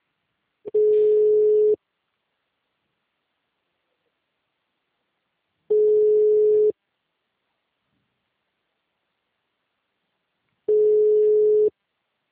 Ringback tone (Europe, 425 Hz).
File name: out- -2022- - - .44861